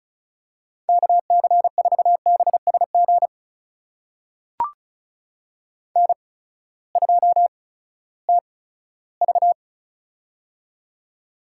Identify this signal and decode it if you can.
Morse code, heard as a keyed tone, about 36 wpm, spaced as for 18 wpm, 700 Hz. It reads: KC4BSG N2TV
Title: USA Call Signs - 8x Character Spacing - 35wpm